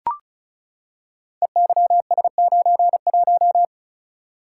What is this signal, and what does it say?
Morse code, heard as a keyed tone, about 36 wpm, 690 Hz. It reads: EYS91
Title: Code Groups - 5 Letter:Numbers - 35wpm